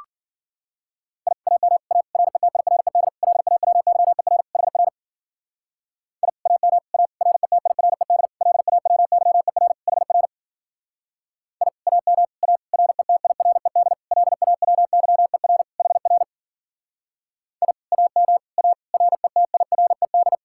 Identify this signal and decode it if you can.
Morse code, heard as a keyed tone, about 52 wpm, 700 Hz. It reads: I AM A RETIRED LAWYER HR I AM A RETIRED LAWYER HR I AM A RETIRED LAWYER HR I AM A RETIRED